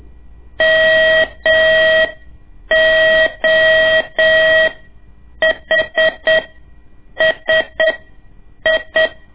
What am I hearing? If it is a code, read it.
Morse code, heard as a keyed tone, about 8 wpm, spaced as for 7 wpm, 640 Hz. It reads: MOHSI
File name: ik1weg2 / CORSI CW / lez2-1